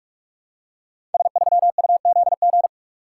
Morse code, heard as a keyed tone, about 44 wpm, 700 Hz. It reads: S3UZG
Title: Code Groups - 5 Letter:Numbers - 45wpm